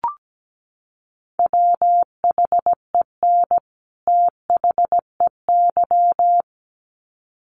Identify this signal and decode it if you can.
Morse code, heard as a keyed tone, about 17 wpm, 700 Hz. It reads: WHEN THEY
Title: Sets of 2 Words 17wpm